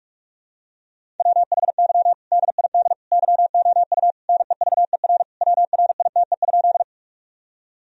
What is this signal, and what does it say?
Morse code, heard as a keyed tone, about 46 wpm, 700 Hz. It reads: WHY DID YOU NEVER WRITE?
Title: Sentences from Top 1000 Words - Review Part 1 - 45wpm